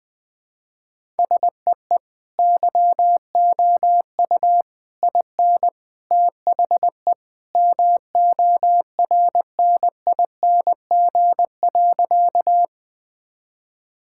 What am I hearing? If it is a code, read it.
Morse code, heard as a keyed tone, about 20 wpm, 700 Hz. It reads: SEE YOU IN THE MORNING.
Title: Sentences with 3-Word Phrases - 20wpm